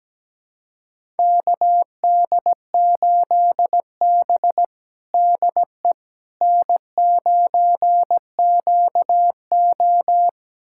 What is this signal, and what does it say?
Morse code, heard as a keyed tone, about 17 wpm, 700 Hz. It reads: KD8B DE N9QO
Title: USA Callsign DE Callsign - 4 Characters - 17wpm